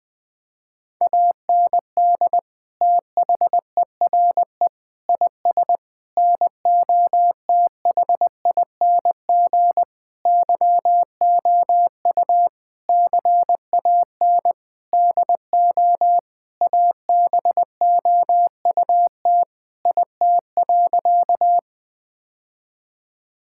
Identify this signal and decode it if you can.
Morse code, heard as a keyed tone, about 20 wpm, 700 Hz. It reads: AND THERE IS NOTHING YOU CAN DO ABOUT IT.
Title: Sentences with 8-Word Phrases - Spoken Beginning - 20wpm